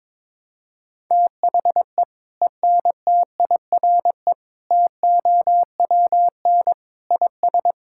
Morse code, heard as a keyed tone, about 22 wpm, 700 Hz. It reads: THE ENTIRE TOWN IS
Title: Sentences from Top 1000 Words - Review Part 5 - 22wpm